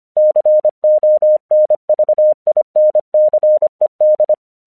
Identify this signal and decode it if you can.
Morse code, heard as a keyed tone, about 25 wpm, 610 Hz. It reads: CONVINCED
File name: 1 English 25wpm / AQC26